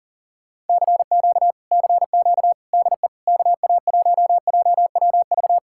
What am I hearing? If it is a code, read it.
Morse code, heard as a keyed tone, about 53 wpm, spaced as for 47 wpm, 700 Hz. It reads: CQ CQ DE KA1JWV
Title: Speed-Racing Common QSO Phrases - 40wpm